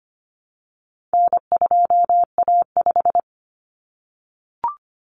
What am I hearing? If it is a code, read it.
Morse code, heard as a keyed tone, about 25 wpm, 700 Hz. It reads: N2A5
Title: Common Cut Numbers - 4-Digit 25wpm